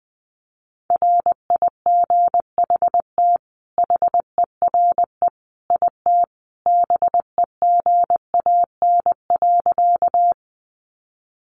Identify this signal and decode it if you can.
Morse code, heard as a keyed tone, about 20 wpm, 700 Hz. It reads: RIGHT HERE IT BEGAN.
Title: Sentences from Top 300 Words 20wpm